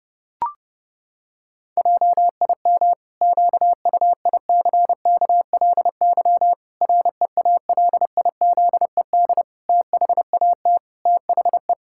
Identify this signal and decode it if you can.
Morse code, heard as a keyed tone, about 30 wpm, 700 Hz. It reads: JIM QUICKLY REALIZED THAT THE
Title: Pangram Challenge v2 - 30wpm